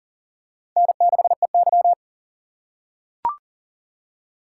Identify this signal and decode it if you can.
Morse code, heard as a keyed tone, about 37 wpm, 700 Hz. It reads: N6EY
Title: USA Call Signs - 4 Characters - 4x Character Spacing - 40wpm